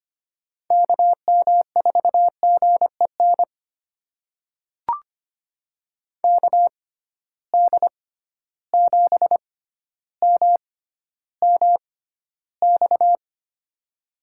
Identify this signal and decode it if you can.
Morse code, heard as a keyed tone, about 25 wpm, spaced as for 13 wpm, 700 Hz. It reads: KM4GEN KD7MMX
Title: USA Call Signs - 6 Characters - 6x Character Spacing - 25wpm